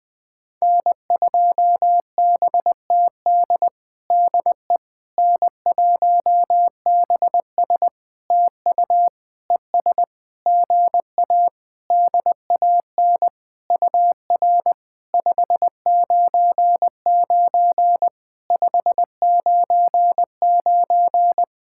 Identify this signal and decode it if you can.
Morse code, heard as a keyed tone, about 20 wpm, 700 Hz. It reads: N2BTD DE N1BS TU ES GA DAN UR 599 599
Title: POTA QSO Practice - 20wpm